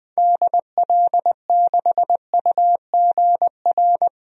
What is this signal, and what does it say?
Morse code, heard as a keyed tone, about 20 wpm, 700 Hz. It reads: DL6UGR
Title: DXCC-100 Call Signs - 6 Characters - Repeated 3x - 20wpm